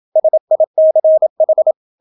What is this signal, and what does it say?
Morse code, heard as a keyed tone, about 27 wpm, 630 Hz. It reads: SICH